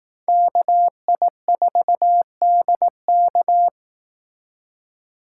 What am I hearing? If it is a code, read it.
Morse code, heard as a keyed tone, about 18 wpm, 700 Hz. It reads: KI4DK
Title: Speed Racing - USA 2x2 Call Signs 15wpm